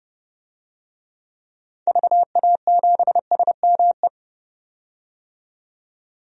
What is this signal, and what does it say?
Morse code, heard as a keyed tone, about 29 wpm, 700 Hz. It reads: VA7SME
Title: DXCC-100 Call Signs - 6 Characters - 30wpm